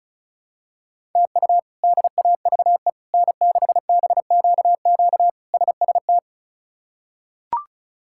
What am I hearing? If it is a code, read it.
Morse code, heard as a keyed tone, about 34 wpm, 700 Hz. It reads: TU DAVE N6BQQ SST
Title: K1USN SST QSO Practice - 35wpm